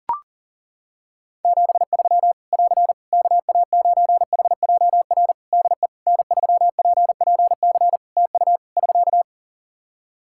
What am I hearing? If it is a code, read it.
Morse code, heard as a keyed tone, about 40 wpm, 700 Hz. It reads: 73 <AR> KA9HJR DE N3PPC TU <SK>
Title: Practice QSO - Exchange 4 - Ending - 40wpm